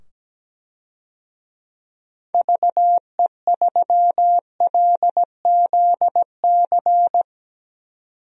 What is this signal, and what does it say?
Morse code, heard as a keyed tone, about 17 wpm, 700 Hz. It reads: VE3LZC
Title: DXCC-100 Call Signs - 6 Characters - 17wpm